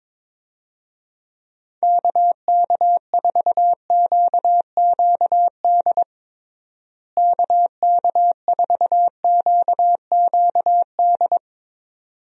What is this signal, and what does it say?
Morse code, heard as a keyed tone, about 22 wpm, 700 Hz. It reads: KK4QQD KK4QQD
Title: USA Call Signs - 6 Characters - Repeated 2x - 22wpm